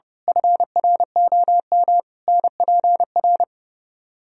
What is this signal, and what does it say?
Morse code, heard as a keyed tone, about 28 wpm, 700 Hz. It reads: FROM NPR
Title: Speed-Racing - Sets of 5 Words 30wpm